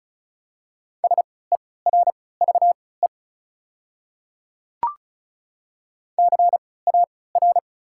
Morse code, heard as a keyed tone, about 36 wpm, spaced as for 20 wpm, 700 Hz. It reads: SERVE CAR
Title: 3x Character Spacing - Top 500 Words - 35wpm